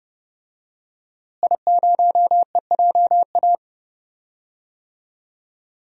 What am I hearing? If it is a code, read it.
Morse code, heard as a keyed tone, about 30 wpm, 700 Hz. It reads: I0EJA
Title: DXCC-100 Call Signs - 5 Characters - Repeated 3x - 30wpm